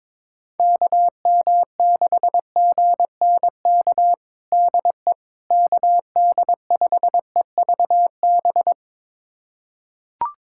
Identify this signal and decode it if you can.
Morse code, heard as a keyed tone, about 22 wpm, 700 Hz. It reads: KM6GNK DE KD5EVB
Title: USA Callsign DE Callsign - 6 Characters - 22wpm